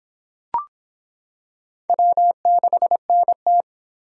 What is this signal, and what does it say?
Morse code, heard as a keyed tone, about 26 wpm, 700 Hz. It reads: W6NT